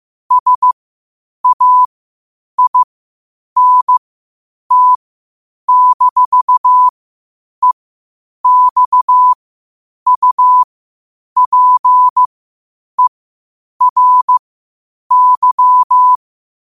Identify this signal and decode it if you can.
Morse code, heard as a keyed tone, about 15 wpm, spaced as for 8 wpm, 1000 Hz. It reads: SAINT-EXUPERY
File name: qotdfiles / 8wpm